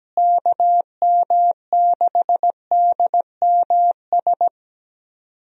Morse code, heard as a keyed tone, about 17 wpm, 700 Hz. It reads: KM6DMS
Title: USA Call Signs - 6 Characters - Repeated 3x - 17wpm